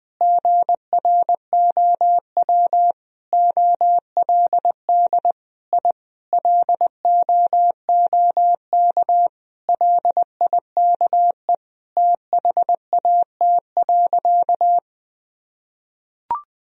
Morse code, heard as a keyed tone, about 20 wpm, 700 Hz. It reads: GROW OLD I LOOK LIKE THAT.
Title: Sentences from Top 1000 Words - Review Part 1 - 20wpm